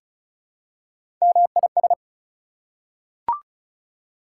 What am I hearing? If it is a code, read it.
Morse code, heard as a keyed tone, about 34 wpm, 700 Hz. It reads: MIS
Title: Prefixes and Suffixes 35wpm